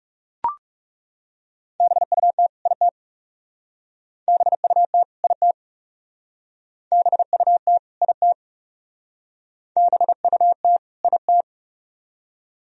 Morse code, heard as a keyed tone, about 39 wpm, 700 Hz. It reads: BUT IT BUT IT BUT IT BUT IT